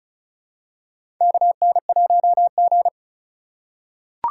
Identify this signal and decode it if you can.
Morse code, heard as a keyed tone, about 34 wpm, 700 Hz. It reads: KN1G
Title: USA Call Signs - 4 Characters - 4x Character Spacing - 35wpm